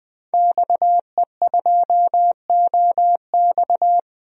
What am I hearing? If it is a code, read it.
Morse code, heard as a keyed tone, about 20 wpm, 700 Hz. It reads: XE2OX
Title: DXCC-100 Call Signs - 5 Characters - 20wpm